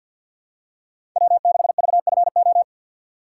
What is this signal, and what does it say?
Morse code, heard as a keyed tone, about 53 wpm, 700 Hz. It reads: W6VFO
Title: USA Call Signs - 5 Characters - 4x Character Spacing - 50wpm